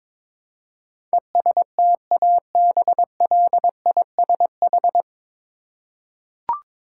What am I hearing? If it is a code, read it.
Morse code, heard as a keyed tone, about 22 wpm, 700 Hz. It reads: ESTABLISH
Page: Morse Code Ninja practice 22wpm